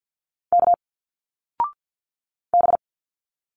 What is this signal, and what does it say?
Morse code, heard as a keyed tone, about 46 wpm, 700 Hz. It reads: K B
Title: Single Letters 50wpm